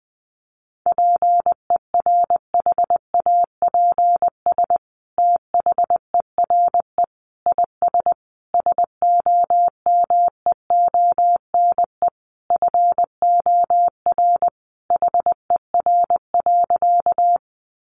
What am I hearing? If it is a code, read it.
Morse code, heard as a keyed tone, about 20 wpm, 700 Hz. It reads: PERHAPS THERE IS SOMEONE FOR HER.
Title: Sentences from Top 600 Words 20wpm